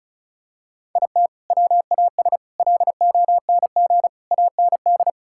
Morse code, heard as a keyed tone, about 36 wpm, 700 Hz. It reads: IT WAS LONG AND
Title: Sentences from Top 1600 Words 35wpm